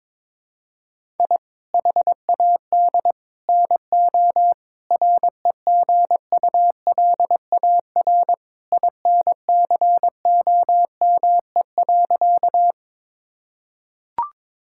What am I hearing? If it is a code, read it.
Morse code, heard as a keyed tone, about 22 wpm, 700 Hz. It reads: I HAD NO REGULAR INCOME.
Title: Sentences from Top 1000 Words - Review Part 6 - 22wpm